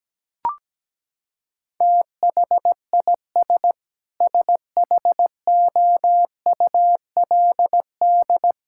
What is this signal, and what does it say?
Morse code, heard as a keyed tone, about 17 wpm, 700 Hz. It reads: THIS SHOULD
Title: Sentences from Top 1000 Words - Review Part 6 - 17wpm